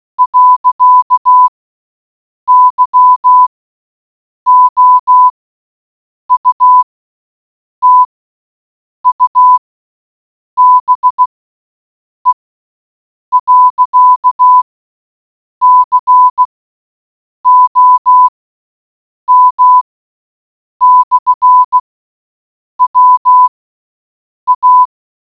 Morse code, heard as a keyed tone, about 16 wpm, spaced as for 7 wpm, 1000 Hz. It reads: .YOUTUBE.COM/WA